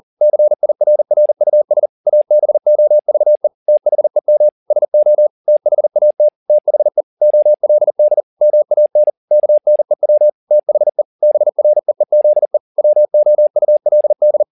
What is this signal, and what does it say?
Morse code, heard as a keyed tone, about 40 wpm, 600 Hz. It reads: CIRRUS ABOVE THEM SO THAT THE OLD MAN KNEW THE BREEZE WOULD
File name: the old man and the sea 2 40wpm